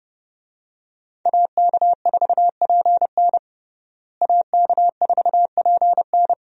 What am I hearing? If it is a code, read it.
Morse code, heard as a keyed tone, about 30 wpm, 700 Hz. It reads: AK4PN AK4PN
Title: USA Call Signs - 5 Characters - Repeated 2x - 30wpm